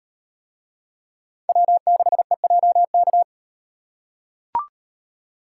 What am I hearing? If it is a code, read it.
Morse code, heard as a keyed tone, about 38 wpm, 700 Hz. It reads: W6EJK